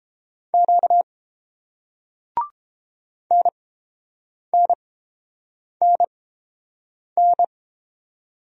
Morse code, heard as a keyed tone, about 30 wpm, spaced as for 7 wpm, 700 Hz. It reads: Q NNNN